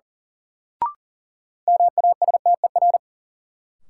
Morse code, heard as a keyed tone, about 38 wpm, 700 Hz. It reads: MASTER
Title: Sentences from Top 1300 Words - New Word Drill - 40wpm